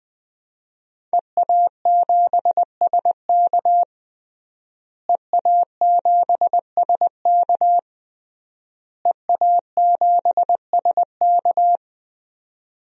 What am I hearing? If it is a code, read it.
Morse code, heard as a keyed tone, about 20 wpm, 700 Hz. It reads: EA7SK EA7SK EA7SK
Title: DXCC-100 Call Signs - 5 Characters - Repeated 3x - 20wpm